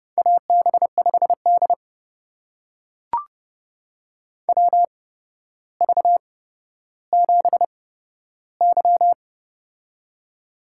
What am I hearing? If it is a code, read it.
Morse code, heard as a keyed tone, about 30 wpm, spaced as for 13 wpm, 700 Hz. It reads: AB5D WV7Y
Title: USA Call Signs - 4 Characters - 8x Character Spacing - 30wpm